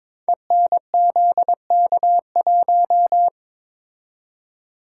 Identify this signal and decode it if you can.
Morse code, heard as a keyed tone, about 22 wpm, 700 Hz. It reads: ENZK1
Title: Code Groups - 5 Letter:Numbers - 22wpm